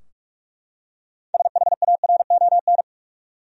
Morse code, heard as a keyed tone, about 45 wpm, 700 Hz. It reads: SHARON
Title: USA Common Names - 45wpm